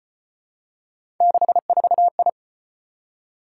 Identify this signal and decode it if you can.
Morse code, heard as a keyed tone, about 33 wpm, 700 Hz. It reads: 64I